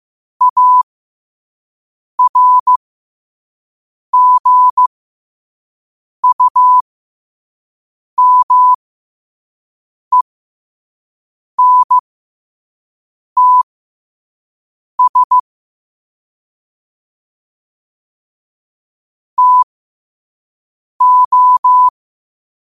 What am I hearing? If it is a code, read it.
Morse code, heard as a keyed tone, about 15 wpm, spaced as for 5 wpm, 1000 Hz. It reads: ARGUMENTS TO
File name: qotdfiles / 5wpm